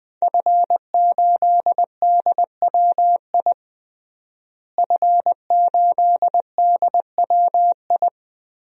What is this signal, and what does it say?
Morse code, heard as a keyed tone, about 20 wpm, 700 Hz. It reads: F8DWI F8DWI
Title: DXCC-100 Call Signs - 5 Characters - Repeated 3x - 20wpm